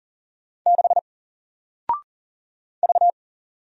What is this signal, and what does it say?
Morse code, heard as a keyed tone, about 40 wpm, 700 Hz. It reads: 6 V
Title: Single Character (Letter:Number:Punctuation:Prosign) Alt 40wpm